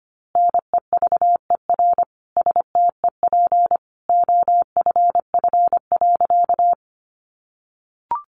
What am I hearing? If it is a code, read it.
Morse code, heard as a keyed tone, about 25 wpm, 700 Hz. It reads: NEVER STEP OFF.
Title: A Sound Of Thunder 25wpm